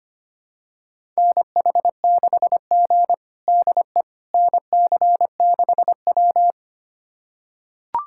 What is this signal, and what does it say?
Morse code, heard as a keyed tone, about 25 wpm, 700 Hz. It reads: NH6G DE NC6W
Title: USA Callsign DE Callsign - 4 Characters - 25wpm